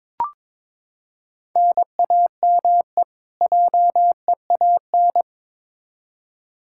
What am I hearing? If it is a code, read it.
Morse code, heard as a keyed tone, about 22 wpm, 700 Hz. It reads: NAME JEAN
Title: Practice QSO - Exchange 1 - Beginning - 22wpm